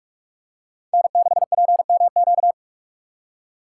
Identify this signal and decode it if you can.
Morse code, heard as a keyed tone, about 46 wpm, 700 Hz. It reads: N6PMQ
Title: USA Call Signs - 5 Characters - 4x Character Spacing - 45wpm